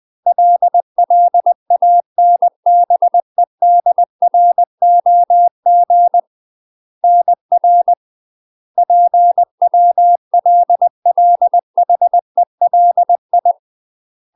Morse code, heard as a keyed tone, about 20 wpm, 700 Hz. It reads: LLANBEDROG NR PWLLHELI
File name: UK MASTER QSOs TRAINING / 20WPM / UK MASTER QSOs 0002